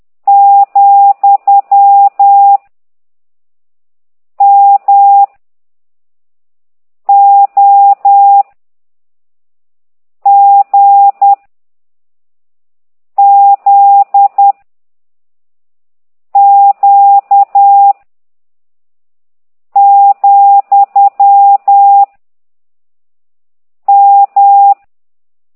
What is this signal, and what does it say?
Morse code, heard as a keyed tone, about 10 wpm, 800 Hz. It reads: , M O G Z Q , M
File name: ct2idw / exerc15